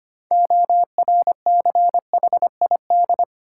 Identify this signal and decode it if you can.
Morse code, heard as a keyed tone, about 25 wpm, 700 Hz. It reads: ORCHID
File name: practice / 25wpm / Difficult Words with Hints - 25wpm